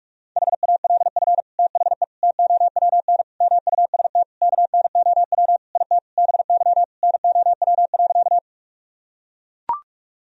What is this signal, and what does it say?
Morse code, heard as a keyed tone, about 41 wpm, 700 Hz. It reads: HALF THE TOWN MUST KNOW IT BY NOW.
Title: Sentences from Top 1000 Words - Review Part 2 - 45wpm